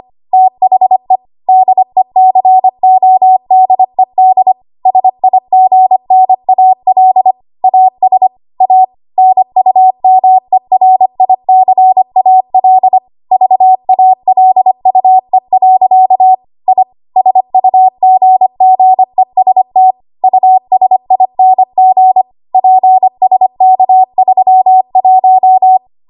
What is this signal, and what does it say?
Morse code, heard as a keyed tone, about 25 wpm, 750 Hz. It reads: THE DECODED SIGNAL AS A NUMERICAL VALUE. I SUGGEST USING PSK31
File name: Archive / 25 WPM / 25WPM